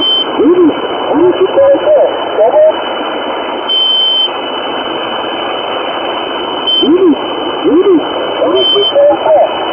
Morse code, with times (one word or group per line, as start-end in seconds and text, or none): none